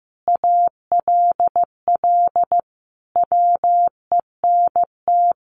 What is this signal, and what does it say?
Morse code, heard as a keyed tone, about 15 wpm, 700 Hz. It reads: ALL WENT
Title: Sentences from Top 1400 Words 15wpm